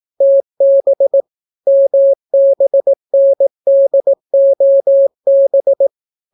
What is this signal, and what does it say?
Morse code, heard as a keyed tone, about 18 wpm, 550 Hz. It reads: TB MBNDOB